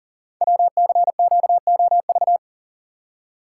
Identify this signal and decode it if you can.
Morse code, heard as a keyed tone, about 42 wpm, 690 Hz. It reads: WCQOV
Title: Code Groups - 5 Letters - 40wpm